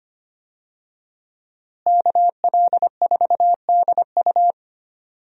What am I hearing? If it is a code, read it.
Morse code, heard as a keyed tone, about 25 wpm, 700 Hz. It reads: KL4DU
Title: USA Call Signs - 5 Characters - Repeated 3x - 25wpm